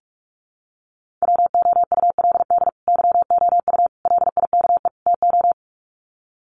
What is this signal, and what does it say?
Morse code, heard as a keyed tone, about 44 wpm, 700 Hz. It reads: WOULD YOU LIKE TO
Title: Sets of 4 Words 45wpm